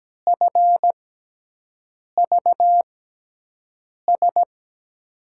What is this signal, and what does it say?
Morse code, heard as a keyed tone, about 17 wpm, spaced as for 6 wpm, 700 Hz. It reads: FVS
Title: USA Call Signs - 6 Characters - 6x Character Spacing - 17wpm